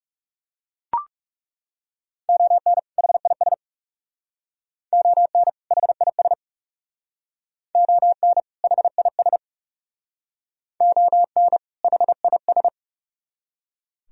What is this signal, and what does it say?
Morse code, heard as a keyed tone, about 37 wpm, 700 Hz. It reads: ON HIS ON HIS ON HIS ON HIS
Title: Speed-Racing - Sets of 2 Words 30wpm